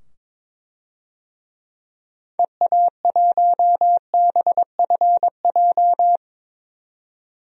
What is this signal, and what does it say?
Morse code, heard as a keyed tone, about 22 wpm, 700 Hz. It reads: EA1BFJ
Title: DXCC-100 Call Signs - 6 Characters - 22wpm